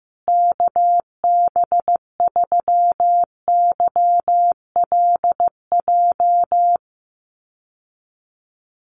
Morse code, heard as a keyed tone, about 15 wpm, 700 Hz. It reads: KB3YLJ
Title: USA Call Signs Repeated 3x 15wpm